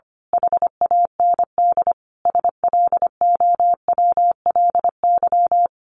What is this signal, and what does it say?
Morse code, heard as a keyed tone, about 25 wpm, 700 Hz. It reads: HAND SLOWLY
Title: A Sound Of Thunder 25wpm